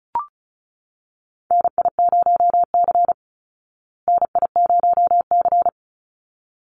Morse code, heard as a keyed tone, about 34 wpm, 700 Hz. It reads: NI0C NI0C
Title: Call Signs Repeated 3x - 35wpm